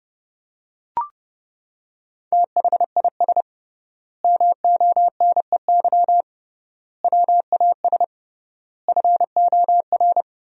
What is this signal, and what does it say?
Morse code, heard as a keyed tone, about 32 wpm, 700 Hz. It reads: THIS MONEY WAS FOR
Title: 3x Word-Spacing - Sentences from Top 400 Words - 30wpm